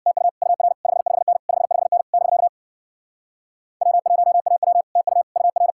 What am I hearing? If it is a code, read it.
Morse code, heard as a keyed tone, about 63 wpm, 700 Hz. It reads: TU UR 55N 55N <BK> W1AW TU UR